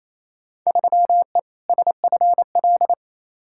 Morse code, heard as a keyed tone, about 28 wpm, 700 Hz. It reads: 3E SFL
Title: Field Day Exchanges Encore - 28wpm